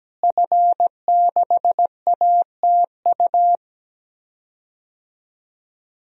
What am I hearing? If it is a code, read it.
Morse code, heard as a keyed tone, about 17 wpm, 700 Hz. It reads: F6ATU
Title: DXCC-100 Call Signs - 5 Characters - Repeated 3x - 17wpm